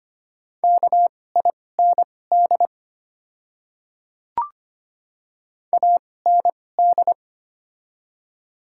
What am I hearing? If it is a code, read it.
Morse code, heard as a keyed tone, about 25 wpm, spaced as for 18 wpm, 700 Hz. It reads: KIND AND
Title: 2x Character Spacing - Top 300 Words - 25wpm